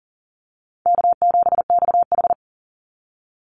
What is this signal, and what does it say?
Morse code, heard as a keyed tone, about 40 wpm, 700 Hz. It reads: K7XH
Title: Easy Call Signs Repeated 3x - 40wpm